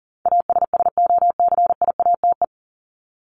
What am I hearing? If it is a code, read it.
Morse code, heard as a keyed tone, about 40 wpm, 700 Hz. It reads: ASSOCIATE
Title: Top 1751-2000 Words 40wpm